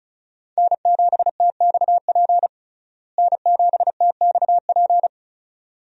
Morse code, heard as a keyed tone, about 34 wpm, 700 Hz. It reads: N7TXP N7TXP
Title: USA Call Signs - 5 Characters - Repeated 2x - 35wpm